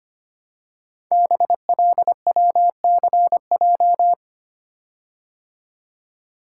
Morse code, heard as a keyed tone, about 25 wpm, 700 Hz. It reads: BLWCJ
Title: Code Groups - 5 Letters - 25wpm